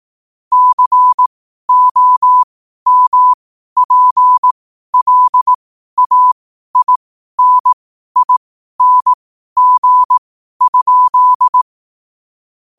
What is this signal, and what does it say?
Morse code, heard as a keyed tone, about 18 wpm, spaced as for 12 wpm, 1000 Hz. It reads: COMPLAINING?
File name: qotdfiles / 12wpm